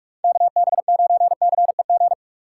Morse code, heard as a keyed tone, about 46 wpm, 700 Hz. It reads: KB9CEG
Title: USA Call Signs - 6 Characters - 6x Character Spacing - 45wpm